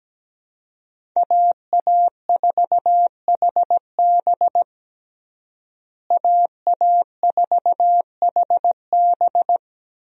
Morse code, heard as a keyed tone, about 17 wpm, 700 Hz. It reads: AA4HB AA4HB
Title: USA Call Signs - 5 Characters - Repeated 2x - 17wpm